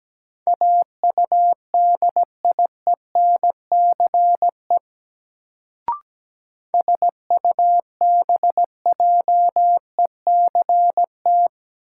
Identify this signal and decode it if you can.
Morse code, heard as a keyed tone, about 17 wpm, 700 Hz. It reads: AUDIENCE SUBJECT
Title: Sentences from Top 800 Words - New Word Drill - 17wpm